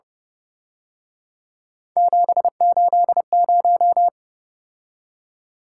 Morse code, heard as a keyed tone, about 30 wpm, 700 Hz. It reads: 780